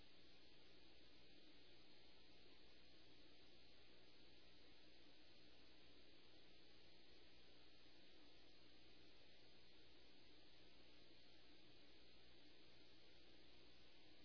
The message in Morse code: none

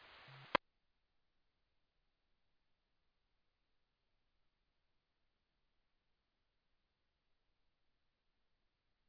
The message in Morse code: none

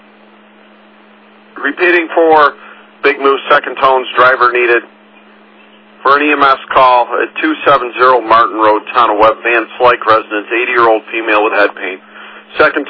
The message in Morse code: none

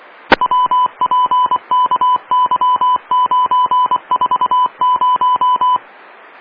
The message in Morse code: WPKY940

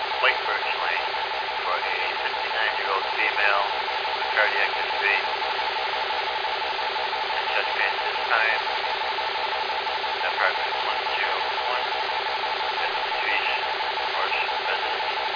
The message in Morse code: none

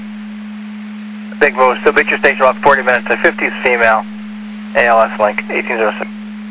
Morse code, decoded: none

